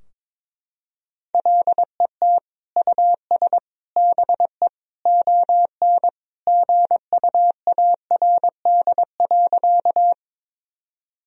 Morse code, heard as a keyed tone, about 22 wpm, 700 Hz. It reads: LET US BE ON GUARD.
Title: Sentences from Top 1000 Words - Review Part 4 - 22wpm